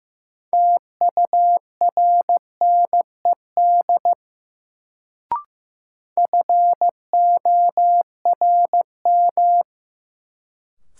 Morse code, heard as a keyed tone, about 15 wpm, 700 Hz. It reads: TURNED FORM
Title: Sets of 4 Words - New Word Drill - 15wpm